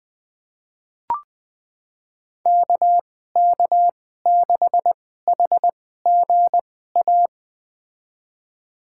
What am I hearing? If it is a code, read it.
Morse code, heard as a keyed tone, about 20 wpm, spaced as for 14 wpm, 700 Hz. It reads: KK6HGA
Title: USA Call Signs - 6 Characters - 2x Character Spacing - 20wpm